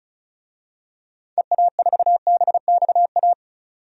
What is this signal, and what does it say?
Morse code, heard as a keyed tone, about 36 wpm, 690 Hz. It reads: EA4BXA
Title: DXCC-100 Call Signs - 6 Characters - 35wpm